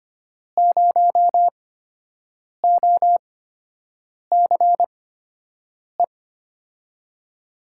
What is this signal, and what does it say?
Morse code, heard as a keyed tone, about 25 wpm, spaced as for 7 wpm, 700 Hz. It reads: 0OCE
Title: USA Call Signs - 8x Character Spacing - 25wpm